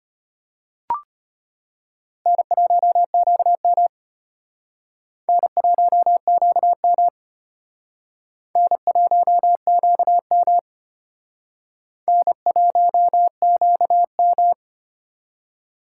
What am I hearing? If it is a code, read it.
Morse code, heard as a keyed tone, about 32 wpm, 700 Hz. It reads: N1QM N1QM N1QM N1QM